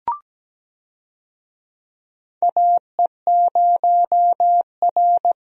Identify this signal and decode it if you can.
Morse code, heard as a keyed tone, about 17 wpm, 700 Hz. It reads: AE0R
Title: USA Call Signs - 4 Characters - 17wpm